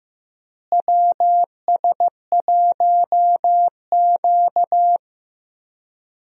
Morse code, heard as a keyed tone, about 15 wpm, 700 Hz. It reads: WS1Q
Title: USA Call Signs - 4 Characters - Repeated 3x - 15wpm